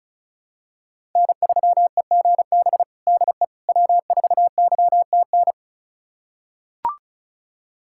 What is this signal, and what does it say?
Morse code, heard as a keyed tone, about 34 wpm, 700 Hz. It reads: N3EGB DE W4YTN
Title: USA Callsign DE Callsign - 5 Characters - 35wpm